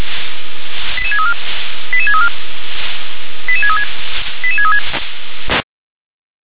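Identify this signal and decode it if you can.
Five-tone selective call (ZVEI), heard as quick runs of five tones, tones of about 70 ms. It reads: 88634 88634 88636 88636
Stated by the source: ZVEI five-tone selective call received on SDR.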